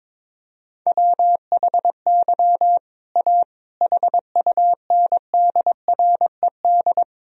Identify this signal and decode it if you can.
Morse code, heard as a keyed tone, about 22 wpm, 700 Hz. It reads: WHY A HUNDRED
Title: Sentences from Top 1000 Words - Review Part 2 - 22wpm